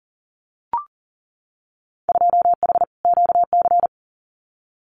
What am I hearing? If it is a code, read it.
Morse code, heard as a keyed tone, about 39 wpm, 700 Hz. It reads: 2H QC
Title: Winter Field Day Exchanges 40wpm